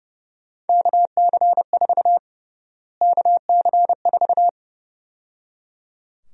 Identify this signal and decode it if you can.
Morse code, heard as a keyed tone, about 30 wpm, 700 Hz. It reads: KC4 KC4
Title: USA Call Sign Prefixes - Repeated 2x - 30wpm